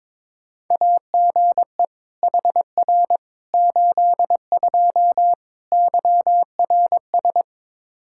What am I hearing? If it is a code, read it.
Morse code, heard as a keyed tone, about 22 wpm, 700 Hz. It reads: AGE HR 82 YRS
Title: K1USN SST QSO Practice 22wpm